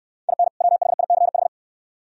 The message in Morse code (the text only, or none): IN WHICH